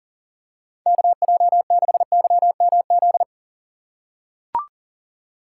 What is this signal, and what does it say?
Morse code, heard as a keyed tone, about 39 wpm, 700 Hz. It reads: KJ6YMZ
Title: USA Call Signs - 6 Characters - 2x Character Spacing - 40wpm